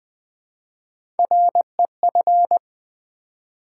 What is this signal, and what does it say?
Morse code, heard as a keyed tone, about 20 wpm, 700 Hz. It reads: REF
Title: Build A Word - 1 Letter At A Time - 5-Letter Words - 20wpm